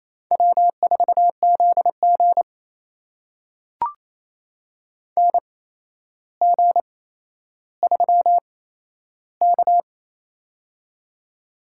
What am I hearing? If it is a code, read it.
Morse code, heard as a keyed tone, about 28 wpm, spaced as for 12 wpm, 700 Hz. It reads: W4ZG NG3K
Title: USA Call Signs - 4 Characters - 8x Character Spacing - 28wpm